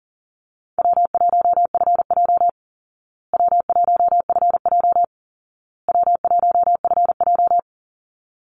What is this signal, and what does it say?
Morse code, heard as a keyed tone, about 40 wpm, 700 Hz. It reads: W1FJ W1FJ W1FJ
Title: Easy Call Signs Repeated 3x - 40wpm